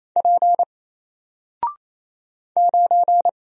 Morse code, heard as a keyed tone, about 28 wpm, 700 Hz. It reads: P 9
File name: Single-Letter-Number-28wpm